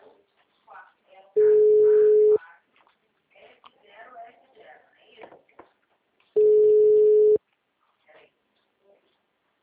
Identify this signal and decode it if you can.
Ringback tone (Europe, 425 Hz), with speech.
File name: out- -2024- - - .18636